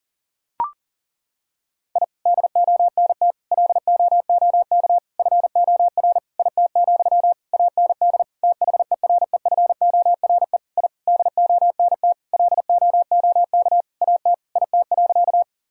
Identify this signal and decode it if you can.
Morse code, heard as a keyed tone, about 37 wpm, 700 Hz. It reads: I DONT LOOK FOR IT, AND THEREFORE I DONT LOOK AT IT.
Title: Sentences from Top 1000 Words - Review Part 5 - 40wpm